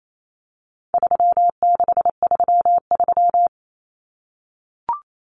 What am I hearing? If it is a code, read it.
Morse code, heard as a keyed tone, about 28 wpm, 700 Hz. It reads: 3633
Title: Common Cut Numbers - 4-Digit 28wpm